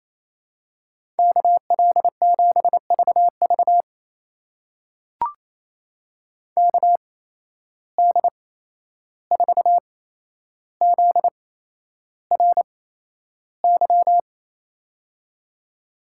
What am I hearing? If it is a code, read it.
Morse code, heard as a keyed tone, about 28 wpm, spaced as for 11 wpm, 700 Hz. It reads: KL7VV KD4ZRY
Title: USA Call Signs - 8x Character Spacing - 28wpm